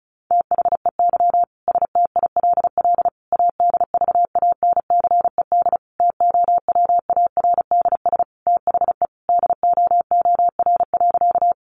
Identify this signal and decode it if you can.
Morse code, heard as a keyed tone, about 34 wpm, 700 Hz. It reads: THEY STILL ADVANCED TOWARDS THE DOOR.
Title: Sentences from Top 1200 Words 35wpm